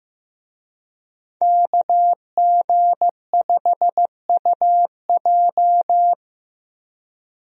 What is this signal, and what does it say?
Morse code, heard as a keyed tone, about 16 wpm, 700 Hz. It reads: KG5UJ
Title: USA Call Signs - 5 Characters - Repeated 3x - 15wpm